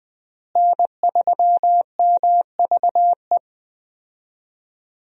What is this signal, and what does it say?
Morse code, heard as a keyed tone, about 20 wpm, 700 Hz. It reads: N3MVE
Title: USA Call Signs - 5 Characters - Repeated 3x - 20wpm